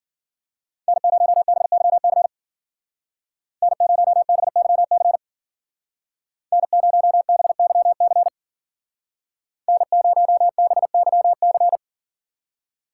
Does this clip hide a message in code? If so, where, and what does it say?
Yes: Morse code from 0.9 to 11.8 s, heard as a keyed tone, about 53 wpm, 700 Hz. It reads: N0BYC N0BYC N0BYC N0BYC